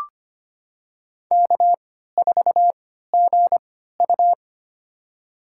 Morse code, heard as a keyed tone, about 25 wpm, spaced as for 14 wpm, 700 Hz. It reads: K4GU
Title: USA Call Signs - 4 Characters - 3x Character Spacing - 25wpm